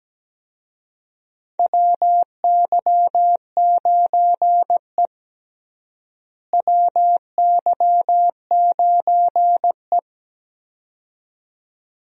Morse code, heard as a keyed tone, about 17 wpm, 700 Hz. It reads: WY9E WY9E
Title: USA Call Signs - 4 Characters - Repeated 2x - 17wpm